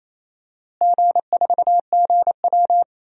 Morse code, heard as a keyed tone, about 27 wpm, 700 Hz. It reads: G4GW
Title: DXCC-100 Call Signs - 4 Characters - Repeated 3x - 28wpm